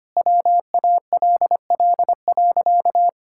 Morse code, heard as a keyed tone, about 25 wpm, 700 Hz. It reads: WALL.
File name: practice / 25wpm / Sentences with 2-Word Phrases - 25wpm